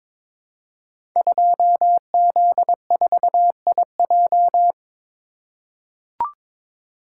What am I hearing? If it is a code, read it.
Morse code, heard as a keyed tone, about 22 wpm, 700 Hz. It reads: 2Z4IJ
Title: Code Groups - 5 Letter:Numbers - 22wpm